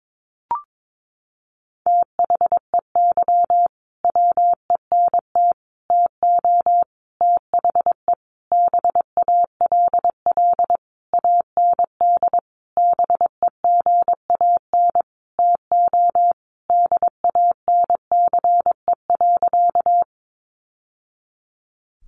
Morse code, heard as a keyed tone, about 22 wpm, 700 Hz. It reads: THEY WENT TO THE BALL AND BEGAN TO DANCE.
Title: Sentences from Top 900 Words 22wpm